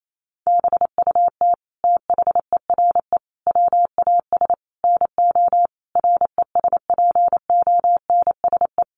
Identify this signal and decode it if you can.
Morse code, heard as a keyed tone, about 28 wpm, 700 Hz. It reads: BUT THERE WAS NO RESPONSE
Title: Sentences from Top 1400 Words 28wpm